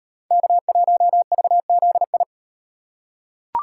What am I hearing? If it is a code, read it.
Morse code, heard as a keyed tone, about 35 wpm, 700 Hz. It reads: K1VZI